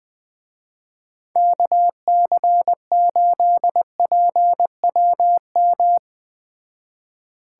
Morse code, heard as a keyed tone, about 21 wpm, 700 Hz. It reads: KC8PWM